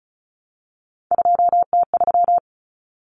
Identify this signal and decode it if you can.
Morse code, heard as a keyed tone, about 34 wpm, 700 Hz. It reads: 2T3
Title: Common Cut Numbers - 3-Digit 35wpm